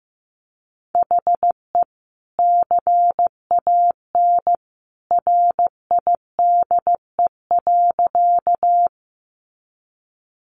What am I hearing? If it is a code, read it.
Morse code, heard as a keyed tone, about 15 wpm, 700 Hz. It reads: HE CAN RIDE.
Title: Sentences from Top 1200 Words 15wpm